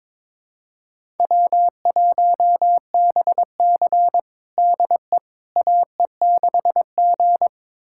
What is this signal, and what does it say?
Morse code, heard as a keyed tone, about 22 wpm, 700 Hz. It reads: W1BC DE AE6G
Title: USA Callsign DE Callsign - 4 Characters - 22wpm